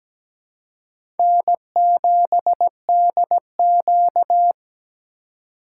Morse code, heard as a keyed tone, about 17 wpm, 700 Hz. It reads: N7DQ
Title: USA Call Signs - 4 Characters - 3x Character Spacing - 17wpm